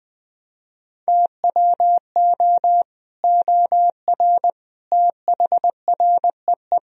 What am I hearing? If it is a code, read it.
Morse code, heard as a keyed tone, about 20 wpm, 700 Hz. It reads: TWO OR THREE
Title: Sentences from Top 1000 Words - Review Part 3 - 20wpm